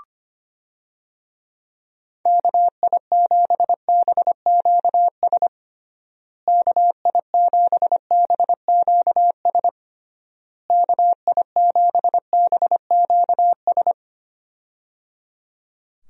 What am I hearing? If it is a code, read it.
Morse code, heard as a keyed tone, about 25 wpm, 700 Hz. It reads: KI7BQS KI7BQS KI7BQS
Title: USA Call Signs - 6 Characters - Repeated 3x - 25wpm